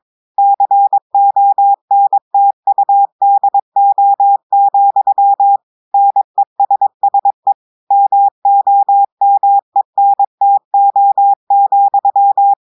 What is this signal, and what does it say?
Morse code, heard as a keyed tone, about 22 wpm, 800 Hz. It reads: CONTUDO, NESSE MOMENTO,
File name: audios / portugais / portugais texte en clair 2 vitesse 22 mots minutes